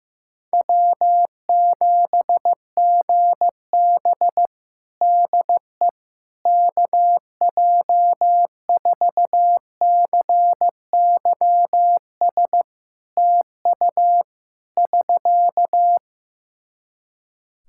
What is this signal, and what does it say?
Morse code, heard as a keyed tone, about 15 wpm, 700 Hz. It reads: W7GB DE KJ4CYS TU <SK>
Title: Practice QSO - Exchange 4 - Ending - 15wpm